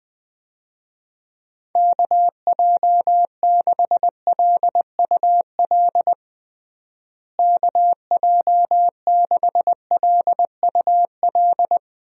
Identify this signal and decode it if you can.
Morse code, heard as a keyed tone, about 21 wpm, 700 Hz. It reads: KJ6LUL KJ6LUL
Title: USA Call Signs - 6 Characters - Repeated 2x - 20wpm